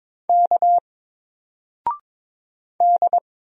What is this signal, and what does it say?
Morse code, heard as a keyed tone, about 22 wpm, 700 Hz. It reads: K D
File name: Single-Letter-Number-22wpm